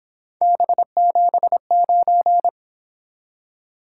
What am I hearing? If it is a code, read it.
Morse code, heard as a keyed tone, about 26 wpm, 700 Hz. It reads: B79
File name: Speed-Racing - Three Letter-Number Combinations - 22wpm